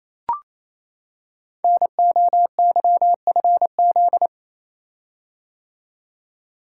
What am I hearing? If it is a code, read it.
Morse code, heard as a keyed tone, about 29 wpm, 700 Hz. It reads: NOYFZ
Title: Code Groups - 5 Letters - 28wpm